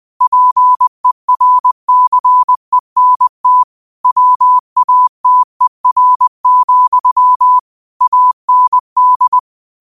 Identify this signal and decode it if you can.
Morse code, heard as a keyed tone, about 20 wpm, 1000 Hz. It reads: PERCENT WATER, AND